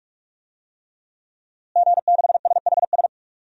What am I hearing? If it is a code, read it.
Morse code, heard as a keyed tone, about 46 wpm, 700 Hz. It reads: G6SHS
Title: DXCC-100 Call Signs - 5 Characters - Repeated 3x - 45wpm